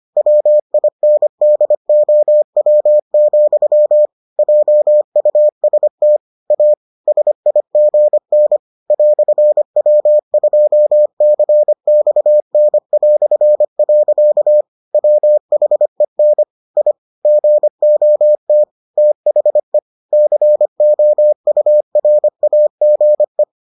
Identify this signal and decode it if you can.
Morse code, heard as a keyed tone, about 25 wpm, 600 Hz. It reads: WINDOW, JUST A SIGN "W2CXN". WHEN I GOT THE COURAGE